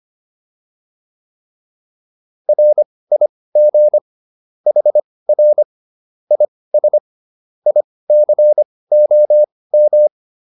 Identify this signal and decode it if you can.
Morse code, heard as a keyed tone, about 25 wpm, spaced as for 18 wpm, 600 Hz. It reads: RIG HR IS ICOM